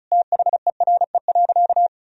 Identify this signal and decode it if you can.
Morse code, heard as a keyed tone, about 36 wpm, 700 Hz. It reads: THERE.
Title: Sentences from Top 1000 Words - Review Part 4 - 35wpm